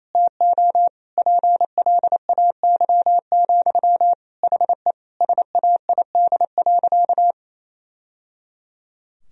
Morse code, heard as a keyed tone, about 28 wpm, 700 Hz. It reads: TO PLAY, HE SAID.
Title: Sentences from Top 1000 Words - Review Part 4 - 28wpm